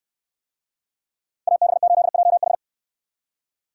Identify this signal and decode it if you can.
Morse code, heard as a keyed tone, about 69 wpm, 690 Hz. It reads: AB8JH